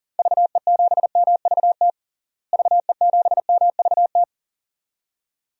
Morse code, heard as a keyed tone, about 40 wpm, 700 Hz. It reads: VE7MVT VE7MVT
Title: DXCC-100 Call Signs - 6 Characters - Repeated 2x - 40wpm